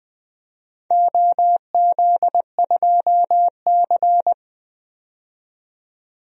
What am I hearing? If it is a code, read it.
Morse code, heard as a keyed tone, about 20 wpm, 700 Hz. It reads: OZ2C
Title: DXCC-100 Call Signs - 4 Characters - 20wpm